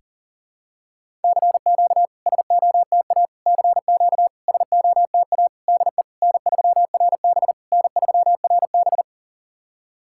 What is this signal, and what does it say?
Morse code, heard as a keyed tone, about 40 wpm, 690 Hz. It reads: CQ SOTA CQ SOTA DE N3RB N3RB
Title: SOTA QSO Practice 40wpm